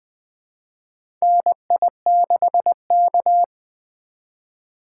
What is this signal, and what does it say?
Morse code, heard as a keyed tone, about 20 wpm, 700 Hz. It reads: NI6K
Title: USA Call Signs - 4x Character Spacing - 20wpm